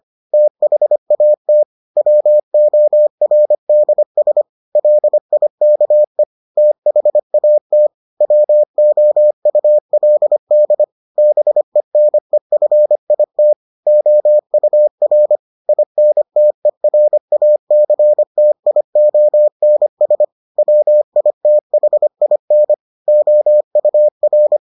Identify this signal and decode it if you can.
Morse code, heard as a keyed tone, about 25 wpm, 600 Hz. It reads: THAT WORDS LIKE THAT WOULD BENEFIT OUR INTERACTIONS WITHIN OUR